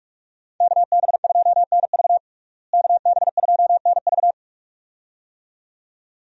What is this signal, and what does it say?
Morse code, heard as a keyed tone, about 46 wpm, 700 Hz. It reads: KB2NV KB2NV
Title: USA Call Signs - 5 Characters - Repeated 2x - 45wpm